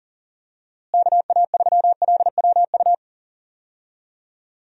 Morse code, heard as a keyed tone, about 37 wpm, 700 Hz. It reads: KA3LWU
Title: USA Call Signs - 6 Characters - Repeated 2x - 40wpm